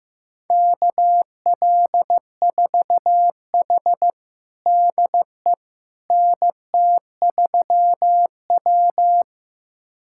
Morse code, heard as a keyed tone, about 15 wpm, 700 Hz. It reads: KL4H DE NT3W
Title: USA Callsign DE Callsign - 4 Characters - 15wpm